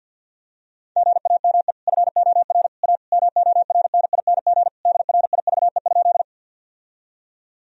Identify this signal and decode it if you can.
Morse code, heard as a keyed tone, about 52 wpm, 700 Hz. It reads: GAME FOR A MORNING DRIVE?
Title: Sentences from Top 1000 Words - Review Part 3 - 50wpm